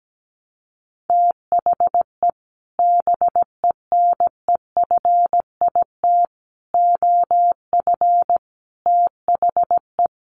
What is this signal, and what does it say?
Morse code, heard as a keyed tone, about 17 wpm, 700 Hz. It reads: THE BENEFIT OF THE
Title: Sets of 5 Words Encore 17wpm